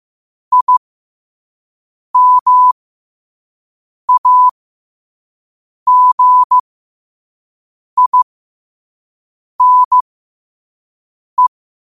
Morse code, heard as a keyed tone, about 15 wpm, spaced as for 5 wpm, 1000 Hz. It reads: IMAGINE
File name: qotdfiles / 5wpm